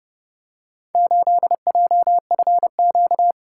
Morse code, heard as a keyed tone, about 26 wpm, 700 Hz. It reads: 8JFQ